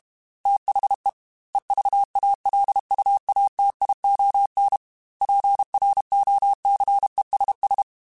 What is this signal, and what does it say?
Morse code, heard as a keyed tone, about 32 wpm, 800 Hz. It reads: THE EVALUATION PROCESS
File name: Technical Subjects / Technical Subjects-mp3s / AND Gate @30